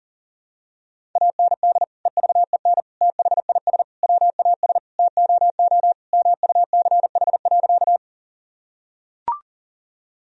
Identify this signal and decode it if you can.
Morse code, heard as a keyed tone, about 37 wpm, 700 Hz. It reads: AND EVEN THIS WAS TOO MUCH.
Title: Sentences from Top 1000 Words - Review Part 1 - 40wpm